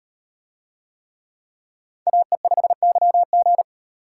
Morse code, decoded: AE5YG

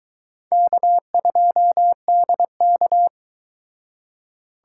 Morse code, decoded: K2DK

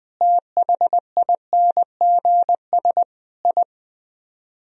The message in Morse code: THINGS I

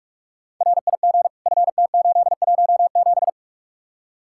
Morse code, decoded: RIG FT817

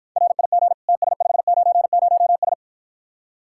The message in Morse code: RIG TS590S

